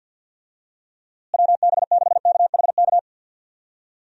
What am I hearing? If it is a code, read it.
Morse code, heard as a keyed tone, about 52 wpm, 700 Hz. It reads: WB6KHK
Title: USA Call Signs - 6 Characters - Repeated 3x - 50wpm